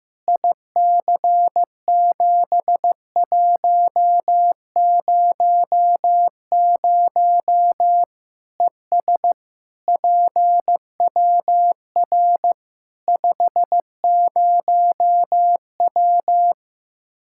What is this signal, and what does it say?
Morse code, heard as a keyed tone, about 16 wpm, 700 Hz. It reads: IC7100 ES PWR 50W
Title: Practice QSO - Exchange 2 - Info - 15wpm